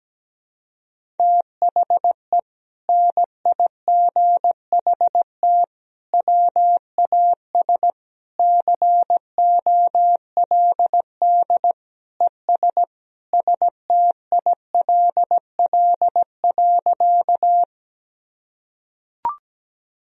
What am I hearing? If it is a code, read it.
Morse code, heard as a keyed tone, about 17 wpm, 700 Hz. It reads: THE NIGHT WAS COLD ES STILL.
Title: Sentences from Top 400 Words and Abbreviations 17wpm